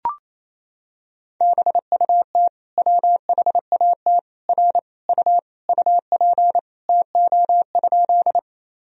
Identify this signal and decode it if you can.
Morse code, heard as a keyed tone, about 28 wpm, 700 Hz. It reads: BUT WHAT R U UP TO?
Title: Sentences from Top 100 Words and Abbreviations 28wpm